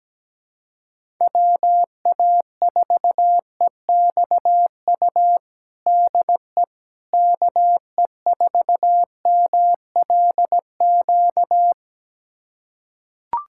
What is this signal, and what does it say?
Morse code, heard as a keyed tone, about 17 wpm, 700 Hz. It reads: WA4EXU DE KE4MLQ
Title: USA Callsign DE Callsign - 6 Characters - 17wpm